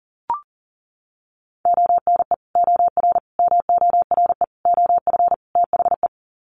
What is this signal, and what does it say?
Morse code, heard as a keyed tone, about 40 wpm, 700 Hz. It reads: ONE OR MORE OF THE
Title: Sets of 5 Words Encore 40wpm